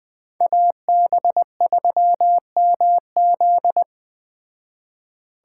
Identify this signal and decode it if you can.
Morse code, heard as a keyed tone, about 20 wpm, 700 Hz. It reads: AB3MZ
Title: Speed Racing - USA Call Signs 17wpm